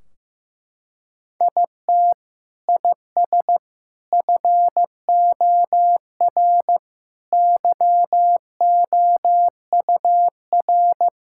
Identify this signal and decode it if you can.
Morse code, heard as a keyed tone, about 15 wpm, 700 Hz. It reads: IT IS FOR YOUR
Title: Sentences from Top 1600 Words 15wpm